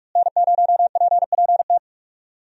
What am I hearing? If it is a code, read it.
Morse code, heard as a keyed tone, about 41 wpm, 700 Hz. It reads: N0PPT
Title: USA Call Signs - 5 Characters - Repeated 3x - 45wpm